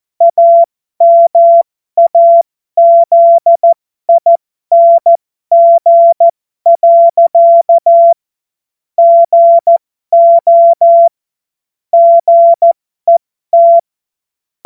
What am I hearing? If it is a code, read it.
Morse code, heard as a keyed tone, about 14 wpm, spaced as for 12 wpm, 680 Hz. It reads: AMAZING. GO GET
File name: Morsekurs / Witze